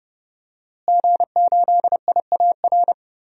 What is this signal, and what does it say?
Morse code, heard as a keyed tone, about 31 wpm, 700 Hz. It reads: G8IAR
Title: DXCC-100 Call Signs - 5 Characters - Repeated 2x - 30wpm